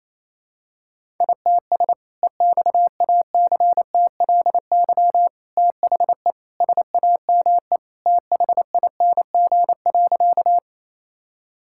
Morse code, heard as a keyed tone, about 28 wpm, 700 Hz. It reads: ITS EXACTLY THE SAME THING.
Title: Sentences from Top 1000 Words - Review Part 6 - 28wpm